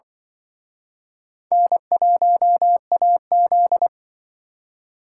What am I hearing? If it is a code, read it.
Morse code, heard as a keyed tone, about 24 wpm, 700 Hz. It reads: N1AZ
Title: Speed Racing - USA 1x2 Call Signs 20wpm